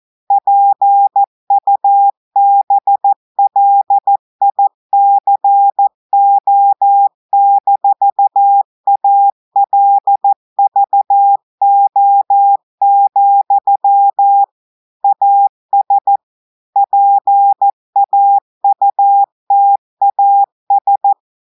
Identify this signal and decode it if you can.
Morse code, heard as a keyed tone, about 14 wpm, 800 Hz. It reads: PUBLICO-ALVO, AS PAUTAS
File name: audios / portugais / portugais texte en clair 2 vitesse 14 mots minutes